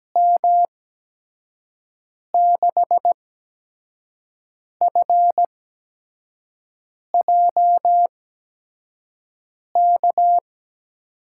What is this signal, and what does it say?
Morse code, heard as a keyed tone, about 17 wpm, spaced as for 5 wpm, 700 Hz. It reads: M6FJK
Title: USA Call Signs - 6 Characters - 8x Character Spacing - 17wpm